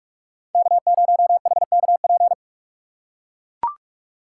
Morse code, heard as a keyed tone, about 40 wpm, 700 Hz. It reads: K0HKP